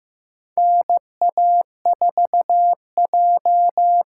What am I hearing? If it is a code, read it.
Morse code, heard as a keyed tone, about 15 wpm, 700 Hz. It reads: NA4J
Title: USA Call Signs - 4 Characters - 4x Character Spacing - 15wpm